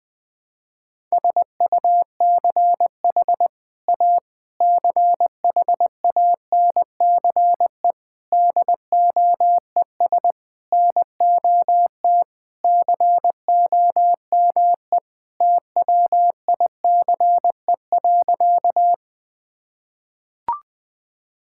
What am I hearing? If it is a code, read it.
Morse code, heard as a keyed tone, about 20 wpm, 700 Hz. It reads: SUCH A CHANCE DOES NOT COME TWICE.
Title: Sentences from Top 1600 Words 20wpm